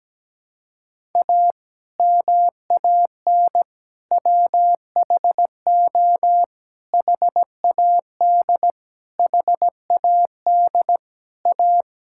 Morse code, heard as a keyed tone, about 17 wpm, 700 Hz. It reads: A MAN WHO HAD HAD A
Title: Sentences from Top 1000 Words - Review Part 6 - 17wpm